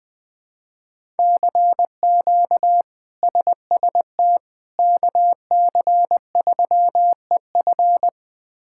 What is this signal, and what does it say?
Morse code, heard as a keyed tone, about 21 wpm, 700 Hz. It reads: CQ SST KC3EF
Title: K1USN SST QSO Practice - 20wpm